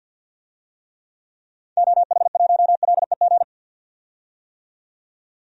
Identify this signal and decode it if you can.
Morse code, heard as a keyed tone, about 48 wpm, 700 Hz. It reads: OH1LEG
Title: DXCC-100 Call Signs - 6 Characters - 50wpm